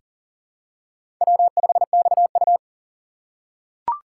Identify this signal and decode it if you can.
Morse code, heard as a keyed tone, about 40 wpm, 700 Hz. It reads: W5XU
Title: K1USN SST QSO Practice - 40wpm